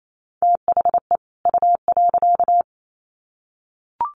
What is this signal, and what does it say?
Morse code, heard as a keyed tone, about 28 wpm, 700 Hz. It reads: THE U.
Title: Sets of 2 Words 28wpm